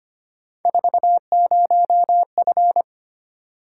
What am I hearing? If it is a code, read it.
Morse code, heard as a keyed tone, about 25 wpm, 700 Hz. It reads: 40F
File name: Three Letter-Number Combinations - 25wpm